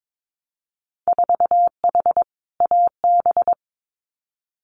Morse code, heard as a keyed tone, about 22 wpm, 700 Hz. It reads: 4H AB
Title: Winter Field Day Exchanges Encore 22wpm